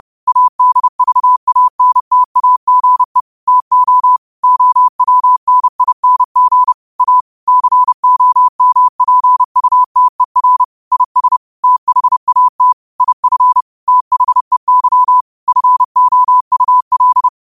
Morse code, heard as a keyed tone, about 31 wpm, 1000 Hz. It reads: ADVANTAGE TO OWNING A COMPUTER IS THAT IF THEY FOUL
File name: qotdfiles / 30wpm